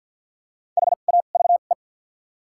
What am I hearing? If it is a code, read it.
Morse code, heard as a keyed tone, about 53 wpm, spaced as for 37 wpm, 700 Hz. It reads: HAVE
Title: 2x Character Spacing - Top 300 Words - 50wpm